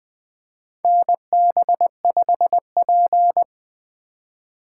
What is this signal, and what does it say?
Morse code, heard as a keyed tone, about 20 wpm, 700 Hz. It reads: NB5P
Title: USA Call Signs - 4 Characters - 3x Character Spacing - 20wpm